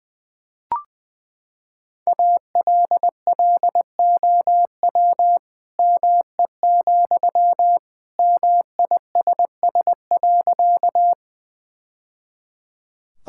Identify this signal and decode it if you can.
Morse code, heard as a keyed tone, about 20 wpm, 700 Hz. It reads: ALLOW ME, MISS.
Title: Sentences from Top 1000 Words - Review Part 5 - 20wpm